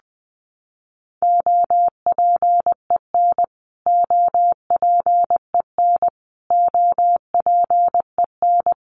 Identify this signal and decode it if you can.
Morse code, heard as a keyed tone, about 20 wpm, 700 Hz. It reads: OPEN OPEN OPEN
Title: Top 500 Words Repeated 20wpm